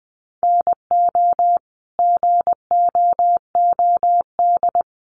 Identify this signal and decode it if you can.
Morse code, heard as a keyed tone, about 20 wpm, 700 Hz. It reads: NO GOOD
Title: Sentences from Top 800 Words 20wpm